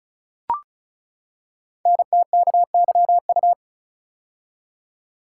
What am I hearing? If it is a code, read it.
Morse code, heard as a keyed tone, about 36 wpm, 690 Hz. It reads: NTKYU
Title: Code Groups - 5 Letters - 35wpm